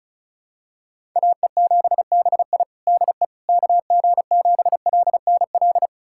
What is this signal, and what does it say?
Morse code, heard as a keyed tone, about 34 wpm, 700 Hz. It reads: AE7BI DE KG7LNL
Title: USA Callsign DE Callsign 35wpm